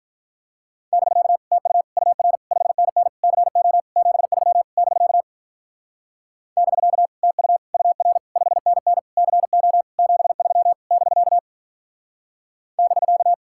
Morse code, heard as a keyed tone, about 42 wpm, 700 Hz. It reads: <BK> TU UR 5NN CO 73 <BK> <BK> TU UR 5NN CO 73 <BK> <BK>
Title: Speed-Racing Common QSO Phrases - 35wpm